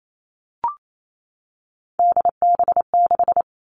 Morse code, heard as a keyed tone, about 28 wpm, 700 Hz. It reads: DB6
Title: All Cut Numbers - 3-Digit 28wpm